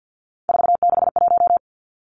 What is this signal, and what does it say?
Morse code, heard as a keyed tone, about 50 wpm, 700 Hz. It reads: 461
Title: Numbers - 3-Digit 50wpm